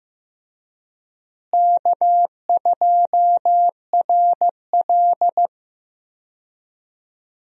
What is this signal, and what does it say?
Morse code, heard as a keyed tone, about 15 wpm, 700 Hz. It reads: K2RL
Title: USA Call Signs - 4 Characters - Repeated 2x - 15wpm